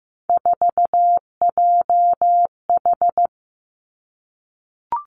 Morse code, heard as a keyed tone, about 15 wpm, 700 Hz. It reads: 4JH
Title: Three Challenging Letter-Number Combinations 15wpm